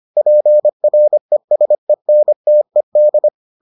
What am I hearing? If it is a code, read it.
Morse code, heard as a keyed tone, about 25 wpm, 600 Hz. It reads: PRESENTED